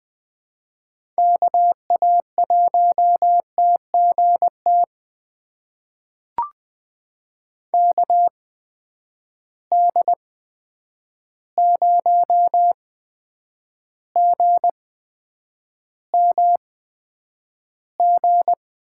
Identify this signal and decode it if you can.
Morse code, heard as a keyed tone, about 21 wpm, spaced as for 9 wpm, 700 Hz. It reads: KA1TGT KD0GMG
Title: USA Call Signs - 6 Characters - 8x Character Spacing - 20wpm